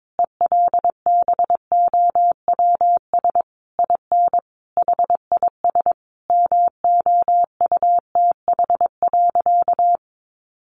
Morse code, heard as a keyed tone, about 22 wpm, 700 Hz. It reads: ELBOWS IN HIS MOUTH.
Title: A Sound Of Thunder 22wpm